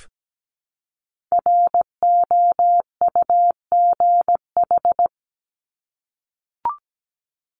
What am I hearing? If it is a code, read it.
Morse code, heard as a keyed tone, about 17 wpm, 700 Hz. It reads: ROUGH
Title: Top 1501-1750 Words 17wpm